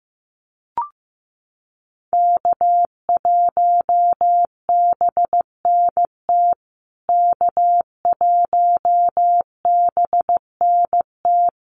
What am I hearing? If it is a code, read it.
Morse code, heard as a keyed tone, about 15 wpm, 700 Hz. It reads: K1BNT K1BNT
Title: USA Call Signs Repeated 3x 15wpm